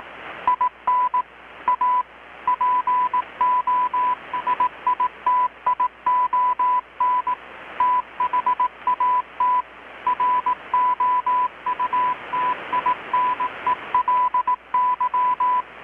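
Morse code, heard as a keyed tone, about 18 wpm, 1000 Hz. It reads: IN A POSITION THAT ROUTINELY
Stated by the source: Morse from KPH's Night of Nights XX maritime radiotelegraphy event, received in Chiapas, Mexico about 3,800 km away on a Kenwood R-600 with a grounded 10 m long-wire antenna.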